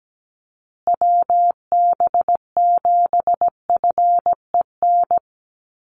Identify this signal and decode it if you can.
Morse code, heard as a keyed tone, about 17 wpm, 700 Hz. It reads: WB7FEN
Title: USA Call Signs Repeated 3x 17wpm